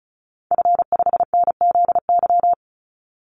Morse code, heard as a keyed tone, about 34 wpm, 700 Hz. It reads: F5NZY
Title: Call Signs - 35wpm